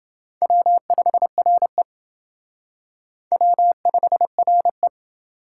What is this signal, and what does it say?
Morse code, heard as a keyed tone, about 28 wpm, 700 Hz. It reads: W5RE W5RE